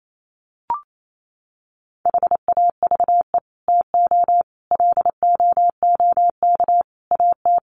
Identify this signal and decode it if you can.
Morse code, heard as a keyed tone, about 28 wpm, 700 Hz. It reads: HAVE TO LOOK AT
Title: Sets of 5 Words Encore 28wpm